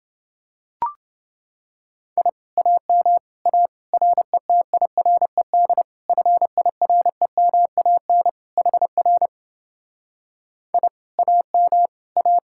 Morse code, heard as a keyed tone, about 30 wpm, 700 Hz. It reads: I AM A RETIRED FIREMAN HR I AM A